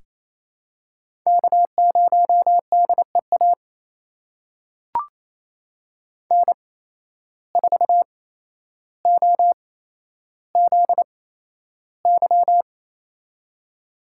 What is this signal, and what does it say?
Morse code, heard as a keyed tone, about 28 wpm, spaced as for 12 wpm, 700 Hz. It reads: K0DEA N4OZY
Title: USA Call Signs - 5 Characters - 8x Character Spacing - 28wpm